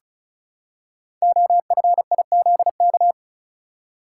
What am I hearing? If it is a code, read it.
Morse code, heard as a keyed tone, about 36 wpm, 690 Hz. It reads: OFIZK
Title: Code Groups - 5 Letters - 35wpm